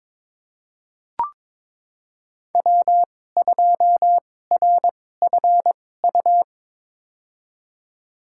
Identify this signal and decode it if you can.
Morse code, heard as a keyed tone, about 22 wpm, spaced as for 16 wpm, 700 Hz. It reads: W2RFU
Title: USA Call Signs - 5 Characters - 2x Character Spacing - 22wpm